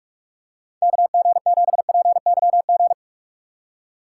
Morse code, heard as a keyed tone, about 46 wpm, 700 Hz. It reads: KG7PYG
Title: USA Call Signs - 6 Characters - 2x Character Spacing - 45wpm